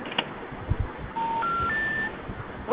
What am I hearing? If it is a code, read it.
Special information tone, with noise.